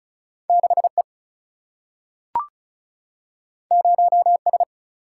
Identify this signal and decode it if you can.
Morse code, heard as a keyed tone, about 36 wpm, 700 Hz. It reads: 6E 0S
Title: Single-Letter Single-Number 35wpm